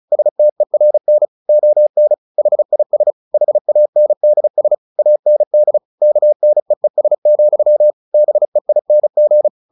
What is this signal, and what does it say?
Morse code, heard as a keyed tone, about 34 wpm, 600 Hz. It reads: STERN ON HIS HANDS AND KNEES, BEING
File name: the old man and the sea 2 35wpm